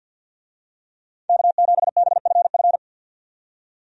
Morse code, heard as a keyed tone, about 53 wpm, 700 Hz. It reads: K7BFF
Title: USA Call Signs - 5 Characters - 50wpm